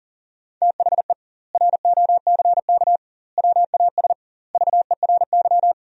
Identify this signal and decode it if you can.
Morse code, heard as a keyed tone, about 40 wpm, 700 Hz. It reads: THE ROCK WAS VERY
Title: 2x Word-Spacing - Sentences from Top 400 Words - 40wpm